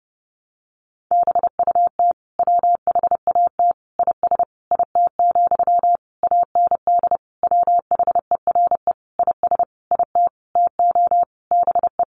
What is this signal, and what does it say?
Morse code, heard as a keyed tone, about 30 wpm, 700 Hz. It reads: BUT WHAT IS IT, AND WHERE IS IT TO BE
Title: Sentences from Top 300 Words 30wpm